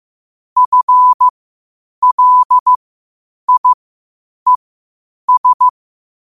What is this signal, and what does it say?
Morse code, heard as a keyed tone, about 15 wpm, spaced as for 8 wpm, 1000 Hz. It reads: FLIES